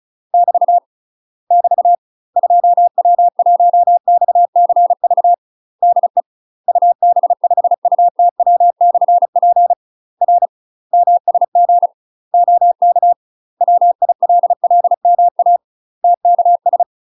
Morse code, heard as a keyed tone, about 36 wpm, 700 Hz. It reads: <BT> <BT> 2W1XCV DE UB5UTW/P R MSG OK WILLMA TKS
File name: UK MASTER QSOs TRAINING / 35WPM / UK MASTER QSOs 0011